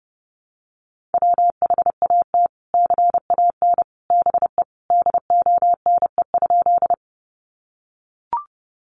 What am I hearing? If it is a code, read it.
Morse code, heard as a keyed tone, about 30 wpm, 700 Hz. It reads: WHAT CAN BE DONE?
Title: Sentences from Top 500 Words 30wpm